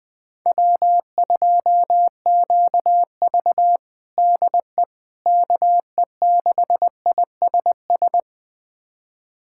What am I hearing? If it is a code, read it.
Morse code, heard as a keyed tone, about 20 wpm, 700 Hz. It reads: W2QV DE KE6ISS
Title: USA Callsign DE Callsign 20wpm